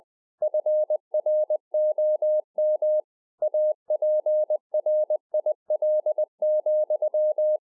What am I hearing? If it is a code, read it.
Morse code, heard as a keyed tone, about 20 wpm, 610 Hz. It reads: FROM APRIL,
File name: 20WPM